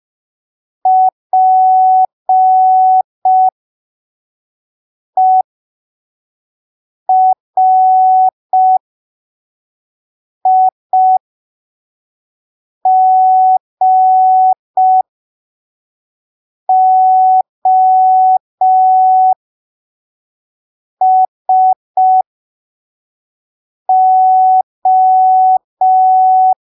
Morse code, heard as a keyed tone, about 5 wpm, spaced as for 3 wpm, 750 Hz. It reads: PERIGOSO
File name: arquivos / texto / texto 24 05ppm